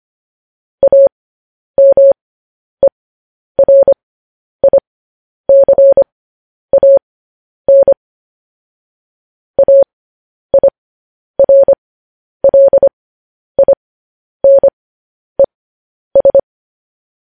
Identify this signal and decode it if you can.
Morse code, heard as a keyed tone, about 25 wpm, spaced as for 10 wpm, 550 Hz. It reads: AMERICAN AIRLINES
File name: podcast / News headlines-1x-25x10